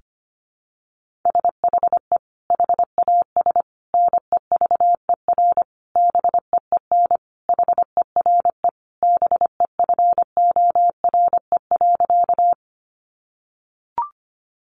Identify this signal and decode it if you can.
Morse code, heard as a keyed tone, about 25 wpm, 700 Hz. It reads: SHE HAS NEVER BEEN HERE BEFORE.
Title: Additional Easy Phrases 25wpm